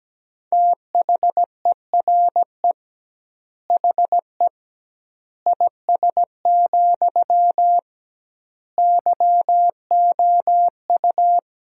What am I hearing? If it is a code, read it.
Morse code, heard as a keyed tone, about 17 wpm, 700 Hz. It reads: THERE HE IS, YOU
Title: 2x Word-Spacing - Sentences from Top 100 Words - 17wpm